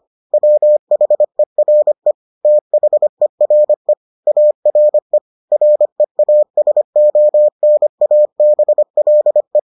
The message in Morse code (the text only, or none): WHERE THERE ARE REASONABLE